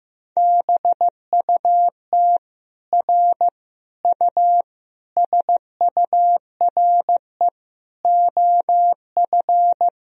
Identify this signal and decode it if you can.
Morse code, heard as a keyed tone, about 16 wpm, 700 Hz. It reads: BUT R U SURE OF
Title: Sentences from Top 300 Words and Abbreviations 15wpm